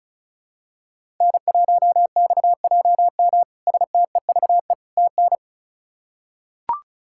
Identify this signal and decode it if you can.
Morse code, heard as a keyed tone, about 36 wpm, 700 Hz. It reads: N1XJM STEVE TN
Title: K1USN SST QSO Practice - 35wpm